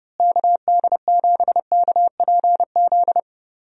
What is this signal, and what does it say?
Morse code, heard as a keyed tone, about 31 wpm, 700 Hz. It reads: KD7KPZ
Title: USA Call Signs - 6 Characters - Repeated 2x - 30wpm